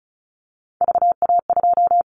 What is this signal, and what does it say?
Morse code, heard as a keyed tone, about 34 wpm, 700 Hz. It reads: VA2
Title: Callsign Prefixes for Top 25 DXCC - Repeated 3x - 35wpm